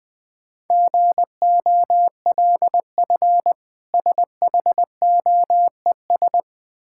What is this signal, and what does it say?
Morse code, heard as a keyed tone, about 21 wpm, 700 Hz. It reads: GOLF SHOES
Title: Difficult Words - Speed-Racing with Hints - 20wpm